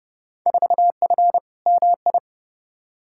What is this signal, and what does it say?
Morse code, heard as a keyed tone, about 30 wpm, 700 Hz. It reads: 4F MI
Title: Field Day Exchanges Encore - 30wpm